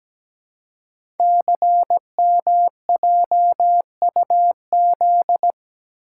Morse code, heard as a keyed tone, about 17 wpm, 700 Hz. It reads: CMJUZ
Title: Code Groups - 5 Letters - 17wpm